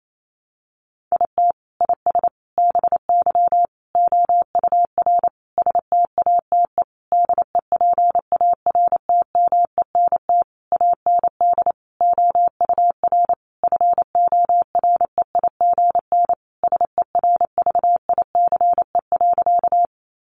Morse code, heard as a keyed tone, about 28 wpm, 700 Hz. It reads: IT IS BY OUR STATE DEPARTMENT AND OUR FOREIGN SERVICE.
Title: Sentences from Top 1400 Words 28wpm